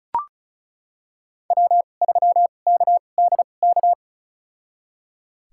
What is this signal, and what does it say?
Morse code, heard as a keyed tone, about 36 wpm, spaced as for 25 wpm, 700 Hz. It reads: W3KDK
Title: USA Call Signs - 5 Characters - 2x Character Spacing - 35wpm